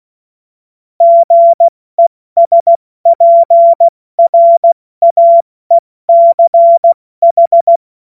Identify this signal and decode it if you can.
Morse code, heard as a keyed tone, about 16 wpm, 680 Hz. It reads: GESPRAECH
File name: Morsekurs / Witze / Witz 04. April 70 F80